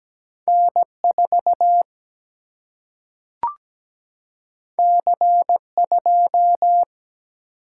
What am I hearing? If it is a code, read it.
Morse code, heard as a keyed tone, about 17 wpm, 700 Hz. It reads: N 4 C 2
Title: Single-Letter Single-Number 17wpm